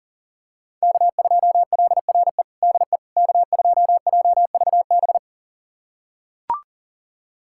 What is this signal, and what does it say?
Morse code, heard as a keyed tone, about 40 wpm, 700 Hz. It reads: K2LRE DE K2JVB
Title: USA Callsign DE Callsign - 5 Characters - 40wpm